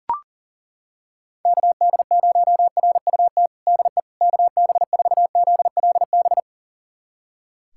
Morse code, heard as a keyed tone, about 40 wpm, 700 Hz. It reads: KD0RUT DE KB4ZLB
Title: USA Callsign DE Callsign - 6 Characters - 40wpm